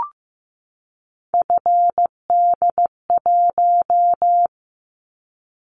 Morse code, heard as a keyed tone, about 15 wpm, 700 Hz. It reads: FD1
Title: Callsign Prefixes for Top 50 DXCC - 15wpm